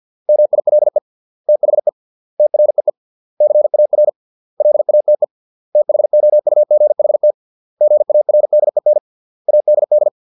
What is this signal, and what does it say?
Morse code, heard as a keyed tone, about 50 wpm, 600 Hz. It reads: MILE THE TREE CAR LATE THOUGHT GARDEN ADD